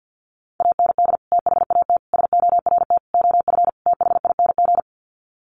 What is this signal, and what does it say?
Morse code, heard as a keyed tone, about 50 wpm, 700 Hz. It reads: AND THAT SORT OF THING
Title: Sets of 5 Words Encore2 50wpm